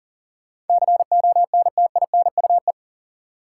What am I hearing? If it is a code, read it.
Morse code, heard as a keyed tone, about 40 wpm, 700 Hz. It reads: CONTINUE
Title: Sentences from Top 1100 Words - New Word Drill Alt - 40wpm